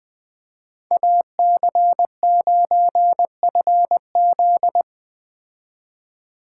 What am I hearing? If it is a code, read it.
Morse code, heard as a keyed tone, about 20 wpm, 700 Hz. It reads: AC9FZ
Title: Speed Racing - USA Call Signs 17wpm